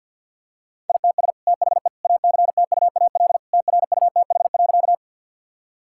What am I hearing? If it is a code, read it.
Morse code, heard as a keyed tone, about 52 wpm, 700 Hz. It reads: ITS THE ACTUAL TRUTH.
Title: Sentences from Top 1000 Words - Review Part 5 - 50wpm